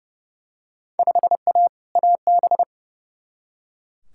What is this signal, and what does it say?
Morse code, heard as a keyed tone, about 30 wpm, 700 Hz. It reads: 5A AB
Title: Field Day Exchanges Encore - 30wpm